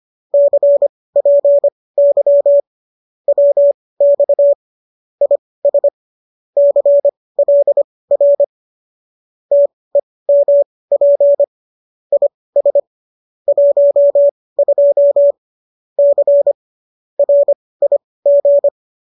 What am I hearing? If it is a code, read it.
Morse code, heard as a keyed tone, about 25 wpm, spaced as for 18 wpm, 570 Hz. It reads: CPY WX IS CLR TEMP IS 12 C RIG